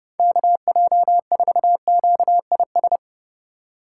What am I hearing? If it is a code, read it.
Morse code, heard as a keyed tone, about 32 wpm, 700 Hz. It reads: KJ4QIS
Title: USA Call Signs - 6 Characters - 3x Character Spacing - 30wpm